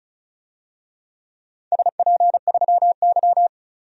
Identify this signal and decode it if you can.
Morse code, heard as a keyed tone, about 34 wpm, 700 Hz. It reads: SP3Y
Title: DXCC-100 Call Signs - 4 Characters - 35wpm